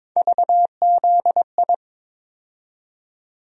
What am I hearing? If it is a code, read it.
Morse code, heard as a keyed tone, about 22 wpm, 700 Hz. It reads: VZI
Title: USA Call Sign Postfixes - 3 Letters 22wpm